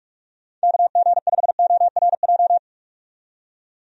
Morse code, heard as a keyed tone, about 43 wpm, 700 Hz. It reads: KG5ORJ
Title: USA Call Signs - 6 Characters - 45wpm